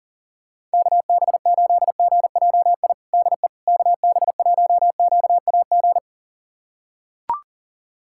Morse code, heard as a keyed tone, about 40 wpm, 700 Hz. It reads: KB8GJI DE KB1QAG
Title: USA Callsign DE Callsign 40wpm